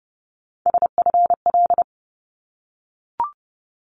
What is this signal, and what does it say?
Morse code, heard as a keyed tone, about 30 wpm, 700 Hz. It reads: SFL
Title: ARRL Sections 30wpm